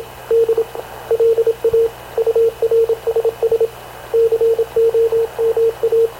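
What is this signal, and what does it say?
Morse code, heard as a keyed tone, about 26 wpm, 460 Hz, with radio noise. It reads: D LA URSS COMA
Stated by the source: Havana Radio / CLA sending Spanish press (PX) in Morse, recorded by Radio Officer Paul Dumesnil onboard CNAV QUEST/CZDO.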